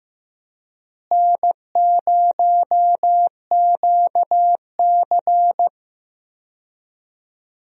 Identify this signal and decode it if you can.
Morse code, heard as a keyed tone, about 15 wpm, 700 Hz. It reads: N0QC
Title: USA Call Signs - 4 Characters - Repeated 3x - 15wpm